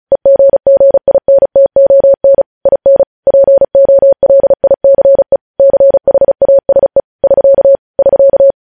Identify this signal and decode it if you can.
Morse code, heard as a keyed tone, about 36 wpm, 550 Hz. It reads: EGGINTON IN POLICE CHASE <SK> <SK>